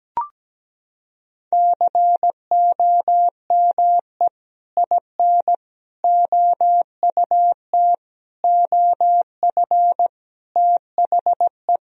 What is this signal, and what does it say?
Morse code, heard as a keyed tone, about 17 wpm, 700 Hz. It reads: COME IN OUT OF THE
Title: Sentences from Top 1000 Words - Review Part 1 - 17wpm